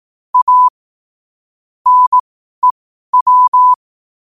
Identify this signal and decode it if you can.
Morse code, heard as a keyed tone, about 18 wpm, spaced as for 12 wpm, 1000 Hz. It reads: A NEW
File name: qotdfiles / 12wpm